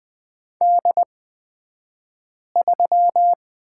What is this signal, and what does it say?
Morse code, heard as a keyed tone, about 20 wpm, spaced as for 5 wpm, 700 Hz. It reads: D3
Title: 242 Lesson 39d - Letter z - Two Characters 20wpm